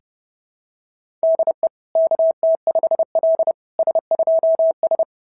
Morse code, heard as a keyed tone, about 30 wpm, 660 Hz. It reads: DE KT5L S2S
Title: SOTA QSO Practice 30wpm